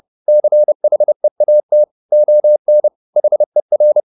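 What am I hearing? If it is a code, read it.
Morse code, heard as a keyed tone, about 30 wpm, 600 Hz. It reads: CHEAT ON HER
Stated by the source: Morse code at 30 wpm, the novel Never by Ken Follett in English, from CW med Gnister.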